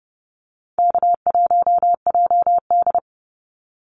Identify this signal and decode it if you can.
Morse code, heard as a keyed tone, about 30 wpm, 700 Hz. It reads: K1JD
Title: NA SOTA Activator Callsigns 30wpm